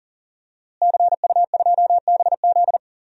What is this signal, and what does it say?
Morse code, heard as a keyed tone, about 40 wpm, 700 Hz. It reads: CU2BZ
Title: DXCC-100 Call Signs - 5 Characters - Repeated 2x - 40wpm